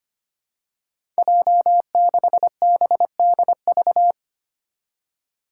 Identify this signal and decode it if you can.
Morse code, heard as a keyed tone, about 25 wpm, 700 Hz. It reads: J6BDV
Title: Five Challenging Letter-Number Combinations 25wpm